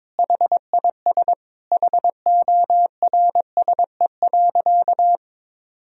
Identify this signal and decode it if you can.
Morse code, heard as a keyed tone, about 22 wpm, 700 Hz. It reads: HIS HORSE.
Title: Sentences from Top 1000 Words - Review Part 3 - 22wpm